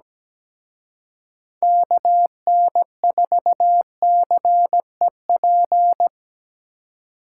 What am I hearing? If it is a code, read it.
Morse code, heard as a keyed tone, about 20 wpm, 700 Hz. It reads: KN4CEP
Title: Speed Racing - USA Call Signs 17wpm